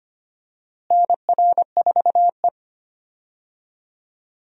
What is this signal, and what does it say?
Morse code, heard as a keyed tone, about 25 wpm, 700 Hz. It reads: NR4E
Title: USA Call Signs - 4 Characters - Repeated 3x - 25wpm